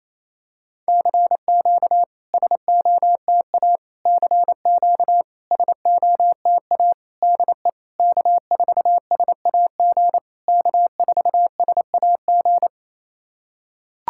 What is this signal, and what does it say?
Morse code, heard as a keyed tone, about 28 wpm, 660 Hz. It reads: CQ SOTA CQ SOTA DE K4SAG K4SAG
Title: SOTA QSO Practice 28wpm